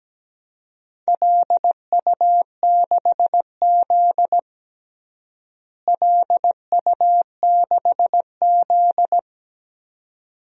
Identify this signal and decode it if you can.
Morse code, heard as a keyed tone, about 17 wpm, 700 Hz. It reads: LU6Z LU6Z
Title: DXCC-100 Call Signs - 4 Characters - Repeated 2x - 17wpm